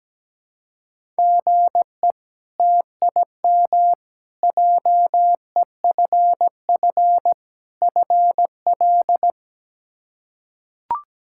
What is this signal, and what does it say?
Morse code, heard as a keyed tone, about 17 wpm, 700 Hz. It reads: GE TIM JEFF FL
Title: K1USN SST QSO Practice - 17wpm